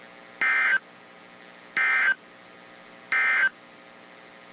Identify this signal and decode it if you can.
Emergency Alert System (SAME) tones, heard as harsh buzzing bursts. It is heard from 0.3 to 3.6 s.
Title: WZ2523 - NOAA Weather Radio